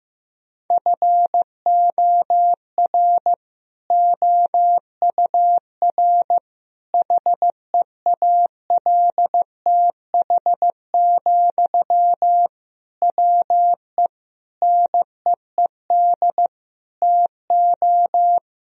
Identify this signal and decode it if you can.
Morse code, heard as a keyed tone, about 15 wpm, 700 Hz. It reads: FOR OUR HEALTH, WE NEED TO
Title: Sentences from Top 1000 Words - Review Part 4 - 15wpm